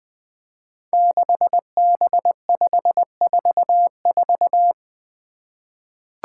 Morse code, heard as a keyed tone, about 20 wpm, 700 Hz. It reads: 6B544
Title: Five Challenging Letter-Number Combinations 20wpm